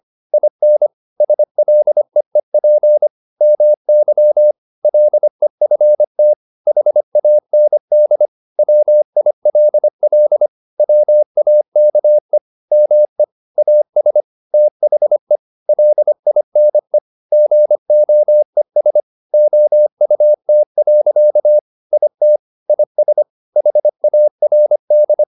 Morse code, heard as a keyed tone, about 25 wpm, 600 Hz. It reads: IN SLEEP MY LEFT HAND WILL WAKE ME AS THE LINE GOES OUT. IT IS HARD